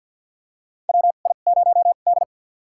Morse code, heard as a keyed tone, about 53 wpm, spaced as for 37 wpm, 700 Hz. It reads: WI0D
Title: USA Call Signs - 4 Characters - 2x Character Spacing - 50wpm